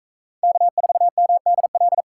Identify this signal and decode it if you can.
Morse code, heard as a keyed tone, about 40 wpm, 700 Hz. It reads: K4MDL